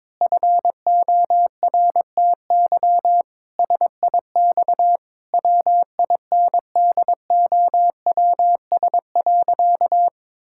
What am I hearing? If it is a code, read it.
Morse code, heard as a keyed tone, about 22 wpm, 700 Hz. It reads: FORTY SIX WINDOWS.
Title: Pangram Challenge v2 - 22wpm